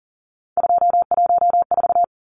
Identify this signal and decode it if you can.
Morse code, heard as a keyed tone, about 40 wpm, 700 Hz. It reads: 214
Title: Numbers - 3-Digit 40wpm